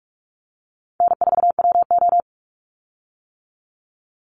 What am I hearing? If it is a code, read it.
Morse code, heard as a keyed tone, about 44 wpm, 700 Hz. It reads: N4WO
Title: Easy Call Signs - 45wpm